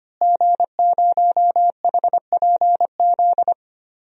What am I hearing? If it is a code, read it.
Morse code, heard as a keyed tone, about 25 wpm, 700 Hz. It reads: G0HPZ
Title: DXCC-100 Call Signs - 5 Characters - Repeated 2x - 25wpm